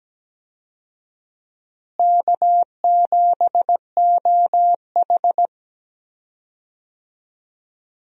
Morse code, heard as a keyed tone, about 17 wpm, 700 Hz. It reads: K7OH
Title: USA Call Signs - 4 Characters - Repeated 3x - 17wpm